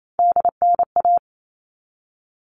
Morse code, heard as a keyed tone, about 28 wpm, 700 Hz. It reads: DNA
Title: All Cut Numbers - 3-Digit 28wpm